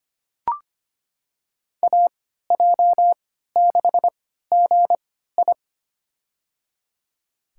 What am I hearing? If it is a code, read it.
Morse code, heard as a keyed tone, about 25 wpm, spaced as for 14 wpm, 700 Hz. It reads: AJ6GI
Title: USA Call Signs - 5 Characters - 3x Character Spacing - 25wpm